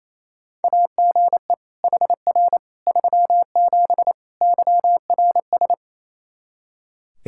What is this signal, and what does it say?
Morse code, heard as a keyed tone, about 28 wpm, 700 Hz. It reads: AGE HR 37 YRS
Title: Practice QSO - Exchange 3 - More Info - 28wpm